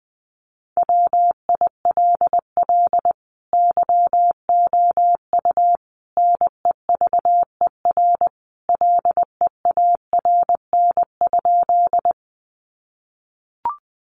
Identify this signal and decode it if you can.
Morse code, heard as a keyed tone, about 20 wpm, 700 Hz. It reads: WILL YOU NEVER LEARN?
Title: Sentences from Top 300 Words 20wpm